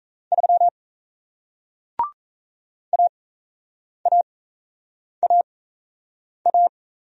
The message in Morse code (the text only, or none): 3 AAAA